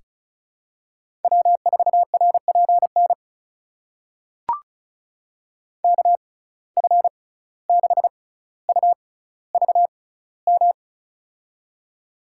Morse code, heard as a keyed tone, about 36 wpm, spaced as for 17 wpm, 700 Hz. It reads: W4RPN KF6UVM
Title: USA Call Signs - 6x Character Spacing - 35wpm